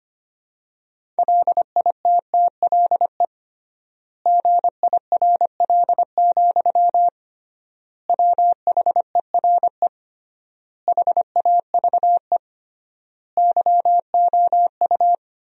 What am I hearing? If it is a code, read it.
Morse code, heard as a keyed tone, about 25 wpm, 700 Hz. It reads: LITTLE GIRL, WHERE HAVE YOU
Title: 3x Word-Spacing - Sentences from Top 400 Words - 25wpm